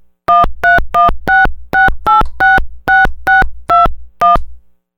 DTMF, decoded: 13166866621